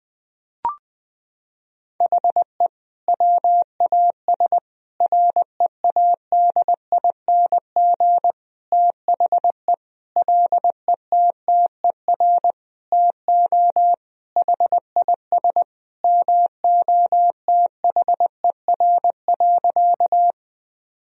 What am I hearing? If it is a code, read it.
Morse code, heard as a keyed tone, about 20 wpm, 700 Hz. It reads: HE WAS READING THE LETTER TO HIS MOTHER.
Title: Sentences from Top 1000 Words - Review Part 3 - 20wpm